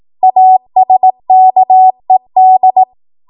Morse code, heard as a keyed tone, about 18 wpm, 750 Hz. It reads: ASKED
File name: Morse / 18WPM